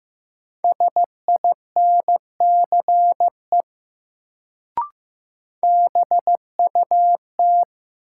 Morse code, heard as a keyed tone, about 15 wpm, 700 Hz. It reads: SINCE BUT
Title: Sets of 2 Words - New Word Drill - 15wpm